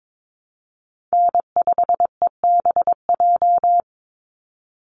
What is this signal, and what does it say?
Morse code, heard as a keyed tone, about 22 wpm, 700 Hz. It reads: N5EBJ
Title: USA Call Signs 22wpm